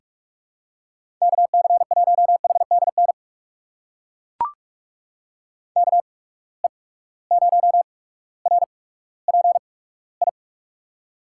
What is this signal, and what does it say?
Morse code, heard as a keyed tone, about 41 wpm, spaced as for 19 wpm, 700 Hz. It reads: KC1HDN KE0RPI
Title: USA Call Signs - 8x Character Spacing - 45wpm